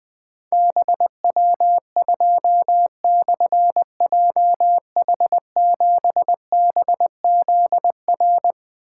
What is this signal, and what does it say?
Morse code, heard as a keyed tone, about 21 wpm, 700 Hz. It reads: BW2/JH7BZR
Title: Difficult Call Signs 20wpm